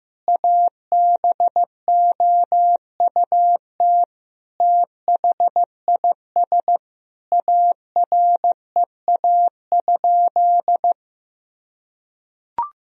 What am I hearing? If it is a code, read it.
Morse code, heard as a keyed tone, about 15 wpm, 700 Hz. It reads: ABOUT THIS AREA?
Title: Sentences from Top 1000 Words - Review Part 2 - 15wpm